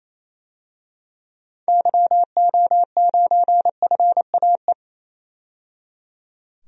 Morse code, heard as a keyed tone, about 28 wpm, 700 Hz. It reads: YO9FAE
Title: DXCC-100 Call Signs - 6 Characters - 28wpm